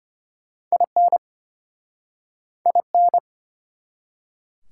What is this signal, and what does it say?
Morse code, heard as a keyed tone, about 33 wpm, 700 Hz. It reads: IN IN